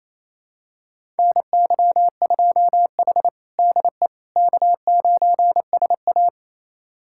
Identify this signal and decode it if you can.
Morse code, heard as a keyed tone, about 28 wpm, 700 Hz. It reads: NY2H DE K9SA
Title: USA Callsign DE Callsign - 4 Characters - 28wpm